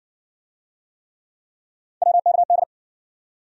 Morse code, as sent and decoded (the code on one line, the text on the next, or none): .-- -.- -..
WKD